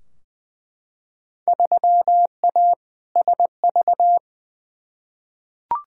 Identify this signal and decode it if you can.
Morse code, heard as a keyed tone, about 20 wpm, 700 Hz. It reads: 3A SV
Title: Field Day Exchanges - 20wpm